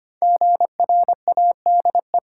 Morse code, heard as a keyed tone, about 25 wpm, 700 Hz. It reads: GRADE
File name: 5 Letter Words - First 3 Letters Given - 25wpm